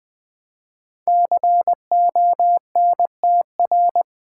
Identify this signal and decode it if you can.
Morse code, heard as a keyed tone, about 20 wpm, 700 Hz. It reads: CONTR
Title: Build A Word - 5 Letters At A Time - 10-Letter Words - 20wpm